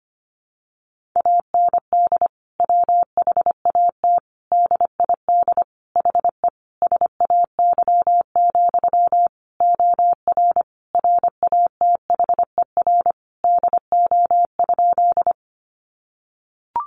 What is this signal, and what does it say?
Morse code, heard as a keyed tone, about 25 wpm, 700 Hz. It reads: AND WHAT DID HE SAY, OR RATHER DO?
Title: Sentences from Top 700 Words 25wpm